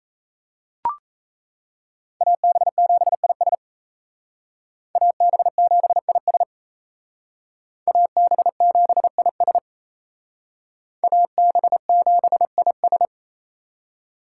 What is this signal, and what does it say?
Morse code, heard as a keyed tone, about 35 wpm, 700 Hz. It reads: AB7IS AB7IS AB7IS AB7IS
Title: Speed Racing - USA Call Signs 28wpm